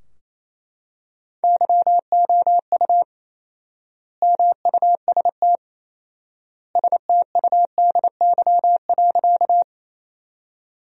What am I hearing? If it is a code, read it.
Morse code, heard as a keyed tone, about 28 wpm, 700 Hz. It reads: YOU MUST STUDY.
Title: 4x Word-Spacing - Sentences from Top 300 Words - 28wpm